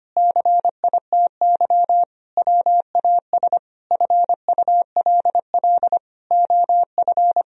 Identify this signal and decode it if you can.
Morse code, heard as a keyed tone, about 25 wpm, 700 Hz. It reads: CITY WAS FULL OF
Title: Sentences from Top 1000 Words - Review Part 4 - 25wpm